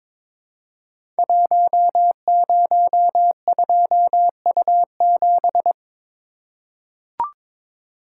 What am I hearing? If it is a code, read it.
Morse code, heard as a keyed tone, about 23 wpm, 700 Hz. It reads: 102U7
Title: Code Groups - 5 Letter:Numbers - 22wpm